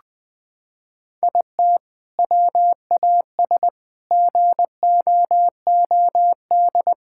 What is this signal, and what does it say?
Morse code, heard as a keyed tone, about 20 wpm, 700 Hz. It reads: IT WAS GOOD
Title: Sentences with 3-Word Phrases - 20wpm